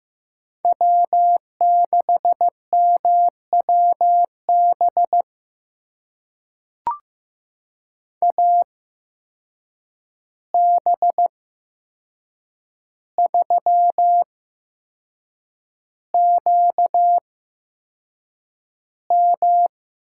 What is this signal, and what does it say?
Morse code, heard as a keyed tone, about 15 wpm, spaced as for 6 wpm, 700 Hz. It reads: W6MWB AB3QM
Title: USA Call Signs - 5 Characters - 8x Character Spacing - 15wpm